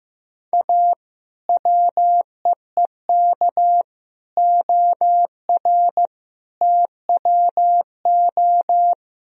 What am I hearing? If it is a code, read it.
Morse code, heard as a keyed tone, about 15 wpm, 700 Hz. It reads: A WEEK OR TWO
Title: Sentences from Top 1000 Words - Review Part 3 - 15wpm